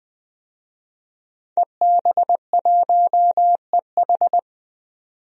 Morse code, heard as a keyed tone, about 21 wpm, 700 Hz. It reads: EB1EH
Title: DXCC-100 Call Signs - 5 Characters - 20wpm